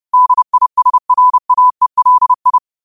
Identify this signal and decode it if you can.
Morse code, heard as a keyed tone, about 30 wpm, 1000 Hz. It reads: DISRAELI
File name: qotdfiles / 30wpm